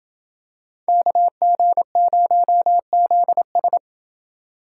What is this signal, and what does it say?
Morse code, heard as a keyed tone, about 28 wpm, 700 Hz. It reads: KG0ZS